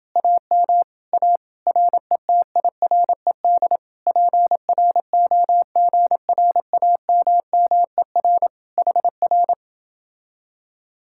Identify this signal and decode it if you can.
Morse code, heard as a keyed tone, about 29 wpm, 700 Hz. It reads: AM A RETIRED PROGRAMMER HR